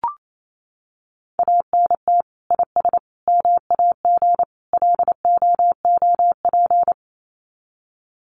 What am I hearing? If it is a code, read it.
Morse code, heard as a keyed tone, about 28 wpm, 700 Hz. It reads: ANT IS MAG LOOP
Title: Common QSO Phrases 28wpm